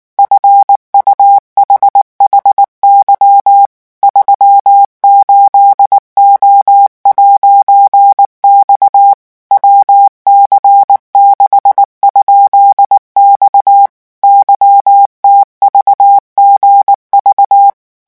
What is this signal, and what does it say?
Morse code, heard as a keyed tone, about 19 wpm, 800 Hz. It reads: FUHHY 38O'X WC6?X YTVGV
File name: morse mp3 / revision14-vitesse 18 mots